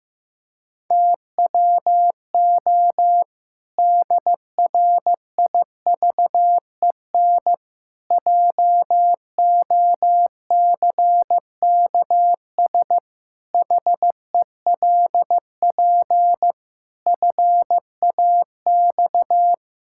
Morse code, heard as a keyed tone, about 15 wpm, 700 Hz. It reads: TWO DRIVEN JOCKS HELP FAX
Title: Pangram Challenge v2 - 15wpm